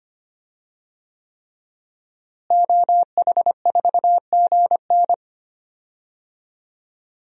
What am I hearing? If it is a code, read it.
Morse code, heard as a keyed tone, about 25 wpm, 700 Hz. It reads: OH4GN
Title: DXCC-100 Call Signs - 5 Characters - Repeated 3x - 25wpm